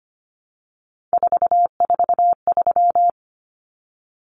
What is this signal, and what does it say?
Morse code, heard as a keyed tone, about 25 wpm, 700 Hz. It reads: 443
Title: Common Cut Numbers - 3-Digit 25wpm